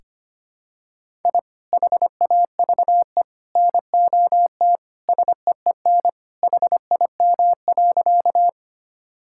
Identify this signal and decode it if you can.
Morse code, heard as a keyed tone, about 25 wpm, 700 Hz. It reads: I HAVE NOT SEEN HIM.
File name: practice / 25wpm / Sentences with 3-Word Phrases - 25wpm